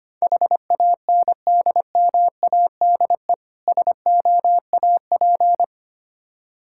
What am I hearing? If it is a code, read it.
Morse code, heard as a keyed tone, about 25 wpm, 700 Hz. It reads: HANDMADE SOAP
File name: practice / 25wpm / Difficult Words with Hints - 25wpm